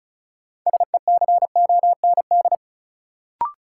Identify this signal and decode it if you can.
Morse code, heard as a keyed tone, about 36 wpm, 700 Hz. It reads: SECOND
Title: Sets of 2 Words - New Word Drill - 35wpm